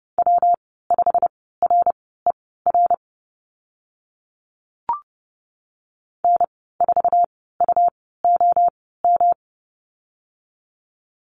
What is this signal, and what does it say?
Morse code, heard as a keyed tone, about 30 wpm, spaced as for 17 wpm, 700 Hz. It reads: W5RER N4UOM
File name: Lesson 19g - Letter w - Call Signs 30wpm